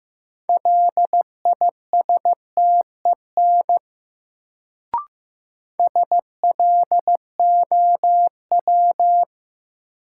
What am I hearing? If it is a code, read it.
Morse code, heard as a keyed tone, about 15 wpm, 700 Hz. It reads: LISTEN SLOW
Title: Sentences from Top 400 Words - New Word Drill - 15wpm